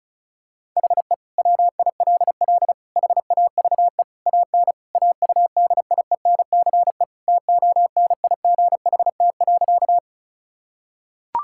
Morse code, heard as a keyed tone, about 36 wpm, 700 Hz. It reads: HE WILL HAVE AN AUDIENCE TONIGHT.
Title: Sentences from Top 1000 Words - Review Part 5 - 35wpm